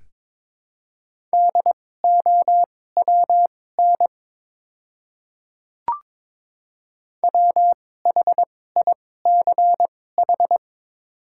Morse code, heard as a keyed tone, about 22 wpm, spaced as for 16 wpm, 700 Hz. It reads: DOWN WHICH
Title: 2x Character Spacing - Top 100 Words - 22wpm